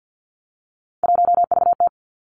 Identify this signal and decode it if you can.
Morse code, heard as a keyed tone, about 50 wpm, 700 Hz. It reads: 1VT
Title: Three Challenging Letter-Number Combinations 50wpm